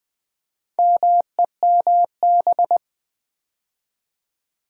Morse code, heard as a keyed tone, about 20 wpm, 700 Hz. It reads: MEMB
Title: Build A Word - 2 Letters At A Time - 6-Letter Words - 20wpm